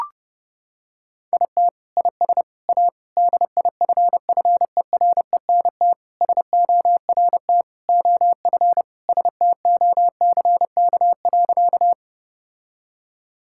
Morse code, heard as a keyed tone, about 30 wpm, 700 Hz. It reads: IT IS A DIFFERENT SORT OF STOCK.
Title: Sentences from Top 1000 Words - Review Part 5 - 30wpm